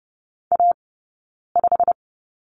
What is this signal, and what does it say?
Morse code, heard as a keyed tone, about 30 wpm, 700 Hz. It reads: A 5
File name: Lesson 20d - Number 3 - Two Characters 30wpm